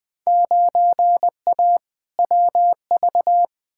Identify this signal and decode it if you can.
Morse code, heard as a keyed tone, about 20 wpm, 700 Hz. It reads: 9A WV
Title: Field Day Exchanges - 20wpm